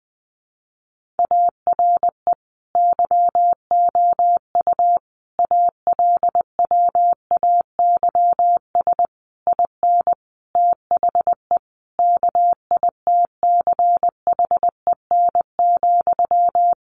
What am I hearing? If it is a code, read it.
Morse code, heard as a keyed tone, about 20 wpm, 700 Hz. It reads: ARE YOU ALWAYS IN THE KITCHEN,
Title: Sentences from Top 1400 Words 20wpm